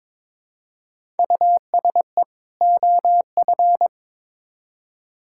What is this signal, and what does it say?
Morse code, heard as a keyed tone, about 21 wpm, 700 Hz. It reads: USE OF
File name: Build A Phrase - 2-Word Phrases - 22wpm